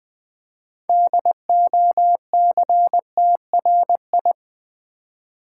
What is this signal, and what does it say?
Morse code, heard as a keyed tone, about 20 wpm, 700 Hz. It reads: DOCTRI
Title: Build A Word - 3 Letters At A Time - 8-Letter Words - 20wpm